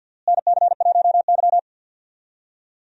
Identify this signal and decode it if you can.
Morse code, heard as a keyed tone, about 52 wpm, 700 Hz. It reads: NC1Y
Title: USA Call Signs - 4 Characters - Repeated 2x - 50wpm